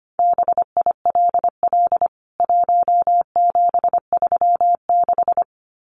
Morse code, heard as a keyed tone, about 25 wpm, 700 Hz. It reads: BILL 1736
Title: CWT Exchange 25wpm